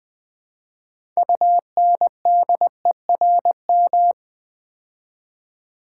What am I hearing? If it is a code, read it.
Morse code, heard as a keyed tone, about 20 wpm, 700 Hz. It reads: UNDERM
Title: Build A Word - 3 Letters At A Time - 9-Letter Words - 20wpm